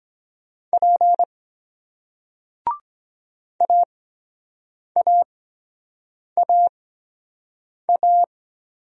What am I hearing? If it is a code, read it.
Morse code, heard as a keyed tone, about 23 wpm, spaced as for 7 wpm, 700 Hz. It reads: P AAAA